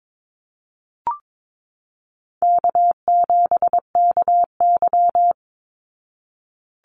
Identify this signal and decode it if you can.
Morse code, heard as a keyed tone, about 22 wpm, 700 Hz. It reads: K7KY
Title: Easy Call Signs Repeated 2x - 22wpm